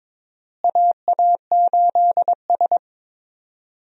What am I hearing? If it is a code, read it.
Morse code, heard as a keyed tone, about 22 wpm, 700 Hz. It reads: AA8S
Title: USA Call Signs - 4 Characters - Repeated 3x - 22wpm